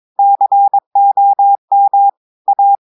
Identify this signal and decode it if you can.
Morse code, heard as a keyed tone, about 22 wpm, 800 Hz. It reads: COM A
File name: audios / portugais / portugais texte en clair 1 vitesse 22 mots minutes